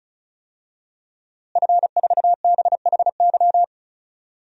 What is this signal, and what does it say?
Morse code, heard as a keyed tone, about 36 wpm, 700 Hz. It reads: F4BHY
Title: DXCC-100 Call Signs - 5 Characters - Repeated 3x - 35wpm